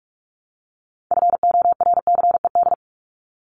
Morse code, heard as a keyed tone, about 44 wpm, 700 Hz. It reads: FORCED
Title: ED Suffix - Top 100 Words with Base Words 45wpm